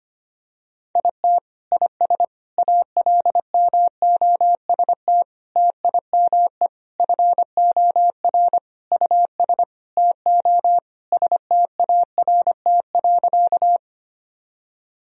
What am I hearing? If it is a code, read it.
Morse code, heard as a keyed tone, about 25 wpm, 700 Hz. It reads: IT IS ALMOST TIME FOR US TO START.
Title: Sentences from Top 1000 Words - Review Part 4 - 25wpm